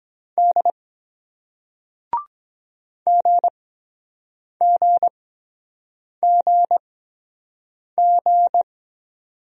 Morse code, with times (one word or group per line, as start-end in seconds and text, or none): 0.4-0.7 D
3.1-8.6 GGGG